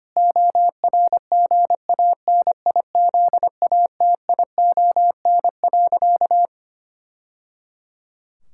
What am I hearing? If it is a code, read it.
Morse code, heard as a keyed tone, about 25 wpm, 700 Hz. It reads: ORGANIZATION.
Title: Sentences from Top 1000 Words - Review Part 5 - 25wpm